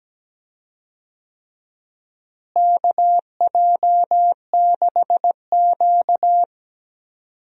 Morse code, heard as a keyed tone, about 17 wpm, 700 Hz. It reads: KJ6Q
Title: USA Call Signs - 4 Characters - Repeated 3x - 17wpm